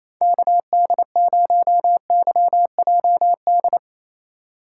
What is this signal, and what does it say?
Morse code, heard as a keyed tone, about 28 wpm, 700 Hz. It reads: KD0YJD
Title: USA Call Signs - 6 Characters - 8x Character Spacing - 28wpm